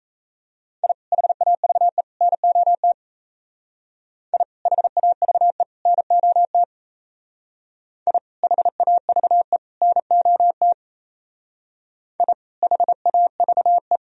Morse code, heard as a keyed tone, about 34 wpm, 700 Hz. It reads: I HAVE NOT I HAVE NOT I HAVE NOT I HAVE
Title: Speed-Racing - Sets of 3 Words 28wpm